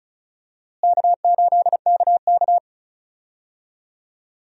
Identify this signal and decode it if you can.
Morse code, heard as a keyed tone, about 34 wpm, 700 Hz. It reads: K8KK
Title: USA Call Signs - 4 Characters - Repeated 2x - 35wpm